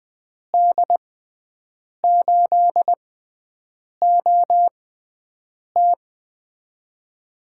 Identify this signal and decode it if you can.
Morse code, heard as a keyed tone, about 20 wpm, spaced as for 7 wpm, 700 Hz. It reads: D8OT
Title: USA Call Signs - 5 Characters - 6x Character Spacing - 20wpm